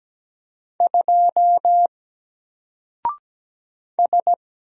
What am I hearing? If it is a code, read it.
Morse code, heard as a keyed tone, about 17 wpm, 700 Hz. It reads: 2 S
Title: Single Character (Letter:Number:Punctuation:Prosign) Alt 17wpm